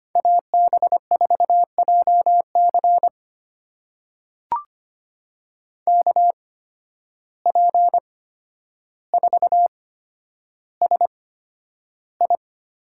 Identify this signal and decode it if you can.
Morse code, heard as a keyed tone, about 25 wpm, spaced as for 11 wpm, 700 Hz. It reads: AB4JC KP4SI
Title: USA Call Signs - 5 Characters - 8x Character Spacing - 25wpm